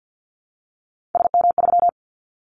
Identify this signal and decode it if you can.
Morse code, heard as a keyed tone, about 50 wpm, 700 Hz. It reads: SM3
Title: Callsign Prefixes for Top 25 DXCC - 50wpm